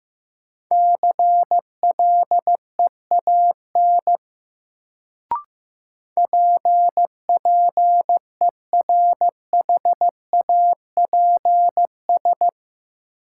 Morse code, heard as a keyed tone, about 15 wpm, 700 Hz. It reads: CLEAN PPERHAPS
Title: Sentences from Top 1100 Words - New Word Drill - 15wpm